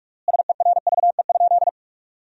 Morse code, heard as a keyed tone, about 46 wpm, 700 Hz. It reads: SERVE?
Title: 3x Word-Spacing - Sentences from Top 400 Words - 45wpm